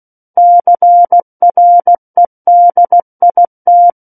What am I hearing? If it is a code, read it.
Morse code, heard as a keyed tone, about 16 wpm, 700 Hz. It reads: CREDIT
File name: Book One / 07-Chapter 7 - 16 WPM